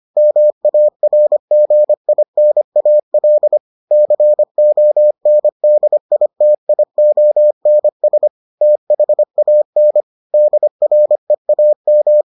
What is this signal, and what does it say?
Morse code, heard as a keyed tone, about 25 wpm, 600 Hz. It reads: MARGINAL CONDITIONS THAN DREAM